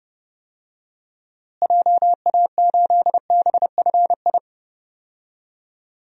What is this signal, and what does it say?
Morse code, heard as a keyed tone, about 30 wpm, 700 Hz. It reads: JA8BFI
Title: DXCC-100 Call Signs - 6 Characters - Repeated 3x - 30wpm